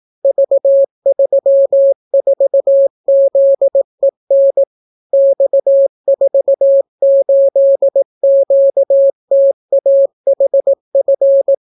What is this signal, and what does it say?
Morse code, heard as a keyed tone, about 18 wpm, 550 Hz. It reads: V34ZEN X48QTAHF